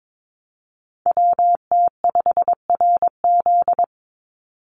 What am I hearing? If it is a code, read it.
Morse code, heard as a keyed tone, about 22 wpm, 700 Hz. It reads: WT5RZ
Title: Call Signs Repeated 3x - 22wpm